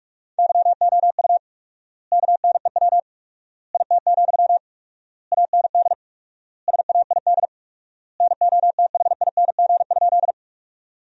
Morse code, heard as a keyed tone, about 42 wpm, 700 Hz. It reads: YOU KNEW IT, AND SAID NOTHING?
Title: 4x Word-Spacing - Sentences from Top 500 Words - 45wpm